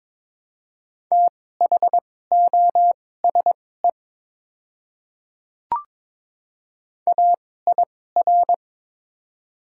Morse code, heard as a keyed tone, about 22 wpm, spaced as for 16 wpm, 700 Hz. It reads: THOSE AIR
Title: 2x Character Spacing - Top 300 Words - 22wpm